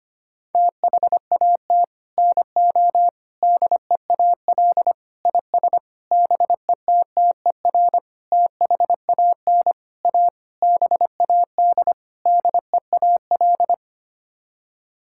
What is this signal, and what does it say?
Morse code, heard as a keyed tone, about 25 wpm, 700 Hz. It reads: THAT NO DEAL IS BETTER THAN A BAD DEAL
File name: Sets of 9 Words - 25wpm